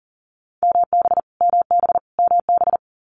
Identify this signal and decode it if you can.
Morse code, heard as a keyed tone, about 40 wpm, 700 Hz. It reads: MB MB MB
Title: Canadian Provinces and Territories - Repeated 3x - 40wpm